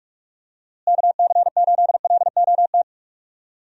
Morse code, heard as a keyed tone, about 43 wpm, 700 Hz. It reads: KC8LOT
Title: USA Call Signs - 6 Characters - Repeated 2x - 45wpm